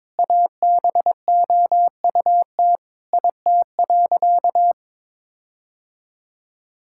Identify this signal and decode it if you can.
Morse code, heard as a keyed tone, about 22 wpm, 700 Hz. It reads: ABOUT IT.
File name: Sentences with 7-Word Phrases - 22wpm